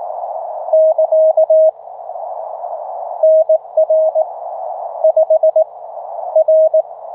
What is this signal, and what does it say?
Morse code, heard as a keyed tone, about 19 wpm, 630 Hz. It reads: <KA> NR 5 R